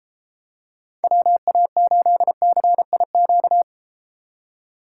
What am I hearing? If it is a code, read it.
Morse code, heard as a keyed tone, about 31 wpm, 700 Hz. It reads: WA8CIQ